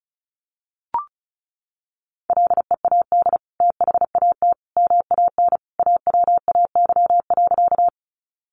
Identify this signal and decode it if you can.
Morse code, heard as a keyed tone, about 34 wpm, 700 Hz. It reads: LEAD THAT MAN AWAY.
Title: Sentences from Top 600 Words 35wpm